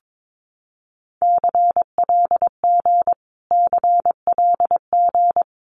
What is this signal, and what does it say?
Morse code, heard as a keyed tone, about 22 wpm, 700 Hz. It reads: CLG CLG
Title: Abbreviations Repeated 3x - 22wpm